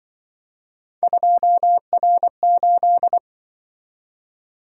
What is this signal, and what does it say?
Morse code, heard as a keyed tone, about 25 wpm, 700 Hz. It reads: 2R8
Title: Speed-Racing - Three Letter-Number Combinations - 20wpm